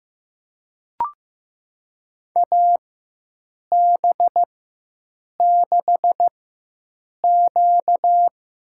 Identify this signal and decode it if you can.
Morse code, heard as a keyed tone, about 15 wpm, spaced as for 7 wpm, 700 Hz. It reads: AB6Q
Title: USA Call Signs - 4 Characters - 4x Character Spacing - 15wpm